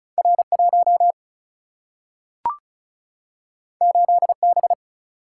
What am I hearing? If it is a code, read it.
Morse code, heard as a keyed tone, about 36 wpm, 700 Hz. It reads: R1 8B
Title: Single-Letter Single-Number 35wpm